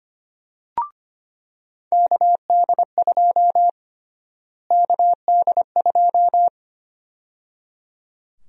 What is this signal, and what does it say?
Morse code, heard as a keyed tone, about 25 wpm, 700 Hz. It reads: KD2 KD2
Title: USA Call Sign Prefixes - Repeated 2x - 25wpm